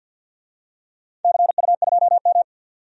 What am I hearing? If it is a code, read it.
Morse code, heard as a keyed tone, about 53 wpm, 700 Hz. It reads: CU2M
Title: DXCC-100 Call Signs - 4 Characters - 50wpm